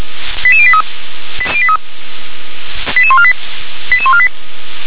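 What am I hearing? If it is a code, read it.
Five-tone selective call (ZVEI), heard as quick runs of five tones, tones of about 70 ms. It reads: 88083 88148 88148
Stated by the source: ZVEI five-tone selective call received on an SDR.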